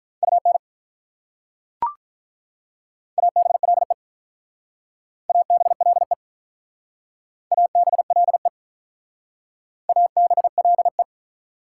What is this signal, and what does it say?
Morse code, heard as a keyed tone, about 44 wpm, 700 Hz. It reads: UN ABLE ABLE ABLE ABLE